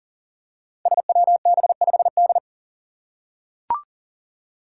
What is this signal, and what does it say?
Morse code, heard as a keyed tone, about 37 wpm, 700 Hz. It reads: SWB5D
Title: Five Challenging Letter-Number Combinations 40wpm